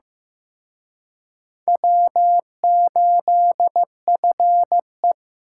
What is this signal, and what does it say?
Morse code, heard as a keyed tone, about 18 wpm, 700 Hz. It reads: W8FE
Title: Speed Racing - USA Call Signs 15wpm